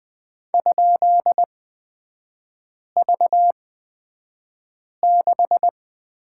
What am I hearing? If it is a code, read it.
Morse code, heard as a keyed tone, about 20 wpm, spaced as for 5 wpm, 700 Hz. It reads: ?V6
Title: 255 Lesson 41e - Period - Three Characters 20wpm